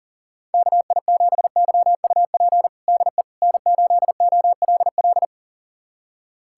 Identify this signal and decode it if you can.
Morse code, heard as a keyed tone, about 40 wpm, 700 Hz. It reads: KI7YUP DE N8OLL
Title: USA Callsign DE Callsign 40wpm